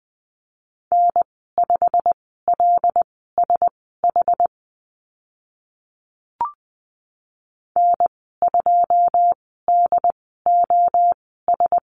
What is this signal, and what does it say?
Morse code, heard as a keyed tone, about 20 wpm, spaced as for 14 wpm, 700 Hz. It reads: N5LSH N2DOS
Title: 088 Lesson 15g - Number 5 - Call Signs 20wpm